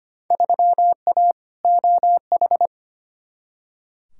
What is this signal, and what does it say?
Morse code, heard as a keyed tone, about 25 wpm, 700 Hz. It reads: 3A OH
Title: Field Day Exchanges - 25wpm